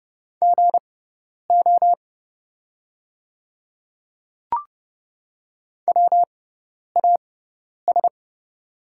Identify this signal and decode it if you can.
Morse code, heard as a keyed tone, about 30 wpm, spaced as for 10 wpm, 690 Hz. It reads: GO WAS
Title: 6x Character Spacing - Top 100 Words - 30wpm